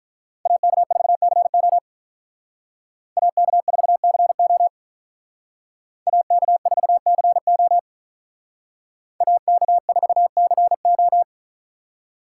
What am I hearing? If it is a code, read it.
Morse code, heard as a keyed tone, about 46 wpm, 700 Hz. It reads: AK4CO AK4CO AK4CO AK4CO